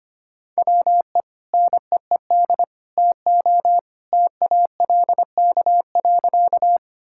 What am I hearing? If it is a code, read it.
Morse code, heard as a keyed tone, about 25 wpm, 700 Hz. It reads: WE NEED TO TALK.
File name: practice / 25wpm / Sentences with 2-Word Phrases - 25wpm